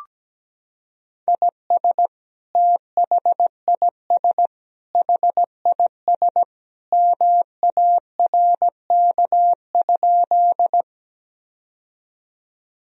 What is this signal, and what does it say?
Morse code, heard as a keyed tone, about 17 wpm, 700 Hz. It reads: IS THIS HIS MARK?
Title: Sentences from Top 1000 Words - Review Part 1 - 17wpm